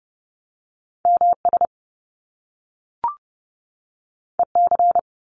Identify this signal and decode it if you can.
Morse code, heard as a keyed tone, about 30 wpm, 700 Hz. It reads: MS EC
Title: Top Two Letter Combinations in Words 30wpm